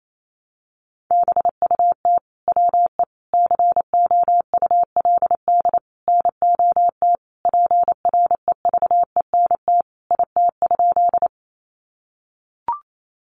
Morse code, heard as a keyed tone, about 28 wpm, 700 Hz. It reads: BUT WE COULD NOT PREVENT IT?
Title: Sentences from Top 1100 Words 28wpm